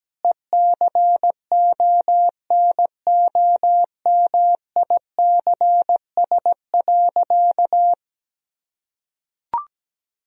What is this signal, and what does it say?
Morse code, heard as a keyed tone, about 17 wpm, 700 Hz. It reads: ECONOMICS.
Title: Sentences from Top 1000 Words - Review Part 3 - 17wpm